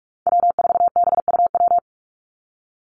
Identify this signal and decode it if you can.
Morse code, heard as a keyed tone, about 44 wpm, 700 Hz. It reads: W4BUW
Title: Call Signs - 45wpm